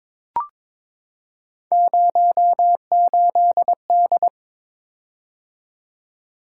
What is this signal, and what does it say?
Morse code, heard as a keyed tone, about 22 wpm, 700 Hz. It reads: 08D
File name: practice / 22wpm / Three Letter-Number Combinations - 22wpm